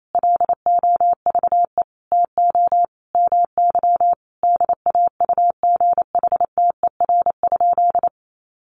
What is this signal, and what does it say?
Morse code, heard as a keyed tone, about 28 wpm, 700 Hz. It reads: LOVE TO MY DAUGHTER?
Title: Sentences from Top 1500 Words 28wpm